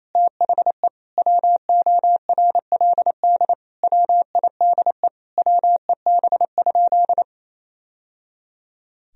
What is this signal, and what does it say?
Morse code, heard as a keyed tone, about 28 wpm, 700 Hz. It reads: THE WORLD WIDE WEB?
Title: Sentences from Top 1000 Words - Review Part 4 - 28wpm